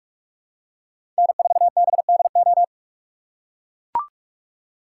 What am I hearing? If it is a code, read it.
Morse code, heard as a keyed tone, about 44 wpm, 700 Hz. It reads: N4BDO
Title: USA Call Signs - 5 Characters - 6x Character Spacing - 45wpm